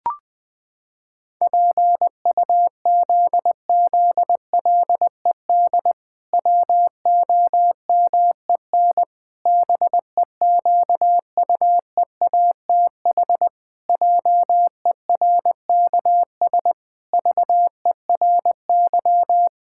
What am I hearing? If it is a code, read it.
Morse code, heard as a keyed tone, about 20 wpm, 700 Hz. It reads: PUZZLED WOMEN BEQUEATH JERKS VERY
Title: Pangram Challenge v2 - 20wpm